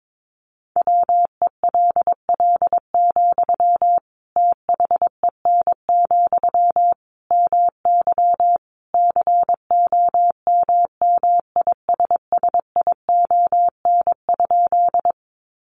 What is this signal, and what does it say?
Morse code, heard as a keyed tone, about 22 wpm, 700 Hz. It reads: WELL, THEN, MY COMMISSION?
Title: Sentences from Top 1400 Words 22wpm